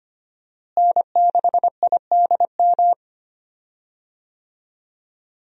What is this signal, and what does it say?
Morse code, heard as a keyed tone, about 25 wpm, 700 Hz. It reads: N6IDM
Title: USA Call Signs - 5 Characters - Repeated 3x - 25wpm